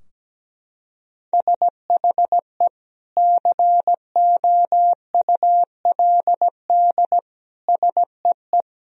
Morse code, heard as a keyed tone, about 17 wpm, 700 Hz. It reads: SHE COULD SEE
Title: Sentences from Top 1000 Words - Review Part 2 - 17wpm